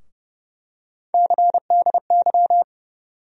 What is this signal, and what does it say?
Morse code, heard as a keyed tone, about 30 wpm, 700 Hz. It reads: CDY
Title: USA Call Sign Postfixes - 3 Letters 30wpm